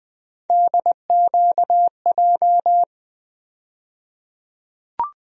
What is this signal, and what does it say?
Morse code, heard as a keyed tone, about 20 wpm, 700 Hz. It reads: DQJ
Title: USA Call Sign Postfixes - 3 Letters 20wpm